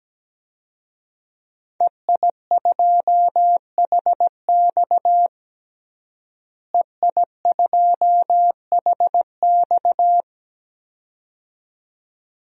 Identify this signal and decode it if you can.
Morse code, heard as a keyed tone, about 17 wpm, 700 Hz. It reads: EI2HX EI2HX
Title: DXCC-100 Call Signs - 5 Characters - Repeated 2x - 17wpm